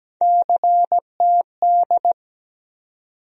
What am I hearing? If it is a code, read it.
Morse code, heard as a keyed tone, about 17 wpm, 700 Hz. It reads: CTD
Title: USA Call Sign Postfixes - 3 Letters 17wpm